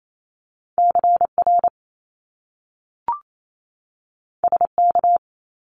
Morse code, heard as a keyed tone, about 28 wpm, 700 Hz. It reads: CR SK
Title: Top Two Letter Combinations in Words 28wpm